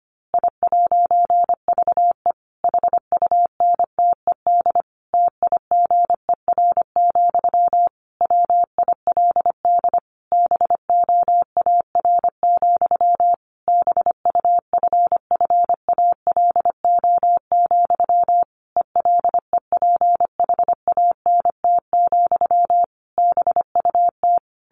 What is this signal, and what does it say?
Morse code, heard as a keyed tone, about 25 wpm, 700 Hz. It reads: I'VE HUNTED TIGER, WILD BOAR, BUFFALO, ELEPHANT, BUT
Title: A Sound Of Thunder 25wpm